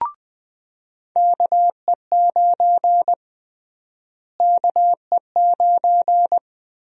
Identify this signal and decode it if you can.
Morse code, heard as a keyed tone, about 21 wpm, 700 Hz. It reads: KE9 KE9
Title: USA Call Sign Prefixes - Repeated 2x - 20wpm